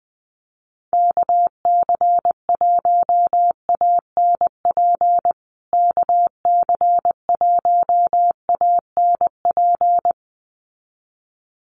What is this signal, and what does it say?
Morse code, heard as a keyed tone, about 20 wpm, 700 Hz. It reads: KC1ANP KC1ANP
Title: USA Call Signs Repeated 2x 20wpm